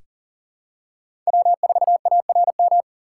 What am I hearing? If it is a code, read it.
Morse code, heard as a keyed tone, about 40 wpm, 700 Hz. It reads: W4ARM
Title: USA Call Signs - 5 Characters - 6x Character Spacing - 40wpm